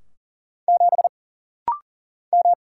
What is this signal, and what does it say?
Morse code, heard as a keyed tone, about 39 wpm, 700 Hz. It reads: 7 M
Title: Single Letter-Number - Rapid-Fire 40wpm